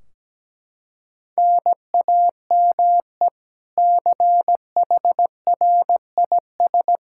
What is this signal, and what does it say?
Morse code, heard as a keyed tone, about 17 wpm, 700 Hz. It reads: NAME CHRIS
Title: Practice QSO - Exchange 1 - Beginning - 17wpm